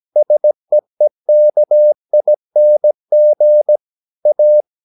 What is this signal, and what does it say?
Morse code, heard as a keyed tone, about 17 wpm, 600 Hz. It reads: SEEKING A
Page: Romanen Never af Ken Follet på engelsk.